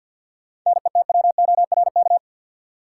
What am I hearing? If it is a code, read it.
Morse code, heard as a keyed tone, about 52 wpm, 700 Hz. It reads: NETWORK
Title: Sentences from Top 1000 Words - Review Part 7 - New Word Drill - 50wpm